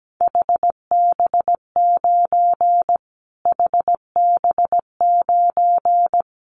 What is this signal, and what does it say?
Morse code, heard as a keyed tone, about 17 wpm, 700 Hz. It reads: HB9 HB9
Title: Callsign Prefixes for Top 25 DXCC - Repeated 3x - 17wpm